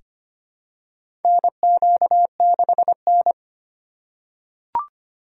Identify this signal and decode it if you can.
Morse code, heard as a keyed tone, about 25 wpm, 700 Hz. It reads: NQ6N
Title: USA Call Signs - 4 Characters - 3x Character Spacing - 25wpm